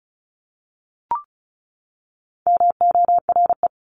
Morse code, heard as a keyed tone, about 34 wpm, 700 Hz. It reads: MORE
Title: Top 300 Words 35wpm